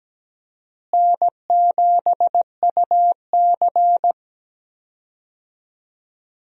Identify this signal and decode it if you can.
Morse code, heard as a keyed tone, about 17 wpm, 700 Hz. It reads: N7UC
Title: USA Call Signs - 4 Characters - Repeated 2x - 17wpm